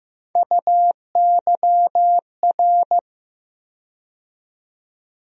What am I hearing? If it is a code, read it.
Morse code, heard as a keyed tone, about 15 wpm, 700 Hz. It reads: UYR
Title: USA Call Sign Postfixes - 3 Letters 15wpm